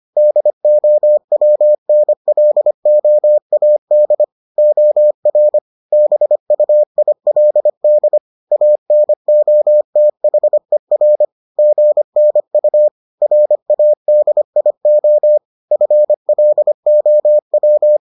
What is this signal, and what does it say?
Morse code, heard as a keyed tone, about 25 wpm, 600 Hz. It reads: DOWNLOAD OR BUILD ANOTHER GNU RADIO FLOW